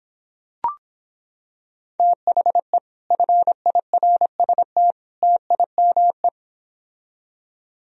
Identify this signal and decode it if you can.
Morse code, heard as a keyed tone, about 26 wpm, 700 Hz. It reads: THE FIRST TIME